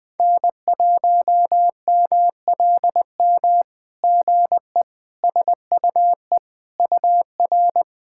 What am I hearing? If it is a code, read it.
Morse code, heard as a keyed tone, about 21 wpm, 700 Hz. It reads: N1MLM GE SUE UR
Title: SOTA QSO Practice 20wpm